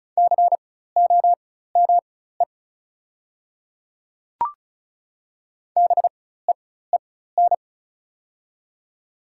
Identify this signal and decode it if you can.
Morse code, heard as a keyed tone, about 33 wpm, spaced as for 16 wpm, 690 Hz. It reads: COME BEEN
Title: 4x Character Spacing - Top 100 Words - 35wpm